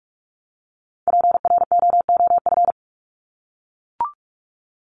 Morse code, heard as a keyed tone, about 44 wpm, 700 Hz. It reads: PROOF
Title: Top 1501-1750 Words 45wpm